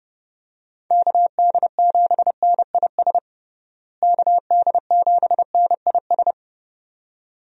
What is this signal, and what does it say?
Morse code, heard as a keyed tone, about 30 wpm, 700 Hz. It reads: KD7NIS KD7NIS
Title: USA Call Signs - 6 Characters - Repeated 2x - 30wpm